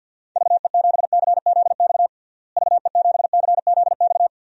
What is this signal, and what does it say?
Morse code, heard as a keyed tone, about 53 wpm, 700 Hz. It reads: VE7CZX VE7CZX
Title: DXCC-100 Call Signs - 6 Characters - Repeated 2x - 50wpm